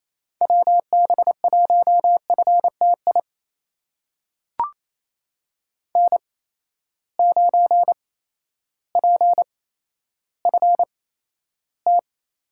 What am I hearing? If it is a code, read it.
Morse code, heard as a keyed tone, about 28 wpm, spaced as for 13 wpm, 700 Hz. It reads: WB1FTI N9PFT
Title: USA Call Signs - 8x Character Spacing - 28wpm